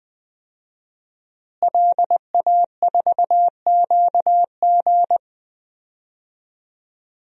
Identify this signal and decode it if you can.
Morse code, heard as a keyed tone, about 20 wpm, 700 Hz. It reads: LA4QG
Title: DXCC-100 Call Signs - 5 Characters - Repeated 2x - 20wpm